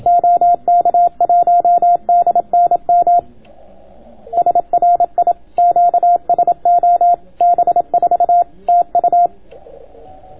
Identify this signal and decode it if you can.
Morse code, heard as a keyed tone, about 27 wpm, 700 Hz. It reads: OK1DNM SRI QSO B4 TU